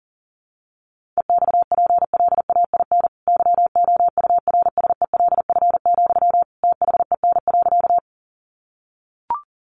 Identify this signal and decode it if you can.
Morse code, heard as a keyed tone, about 40 wpm, 700 Hz. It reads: EXPLAIN YOURSELF, THEN.
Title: Sentences from Top 1100 Words 40wpm